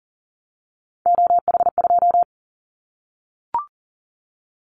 Morse code, heard as a keyed tone, about 40 wpm, 700 Hz. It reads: OH2
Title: Callsign Prefixes for Top 25 DXCC - 40wpm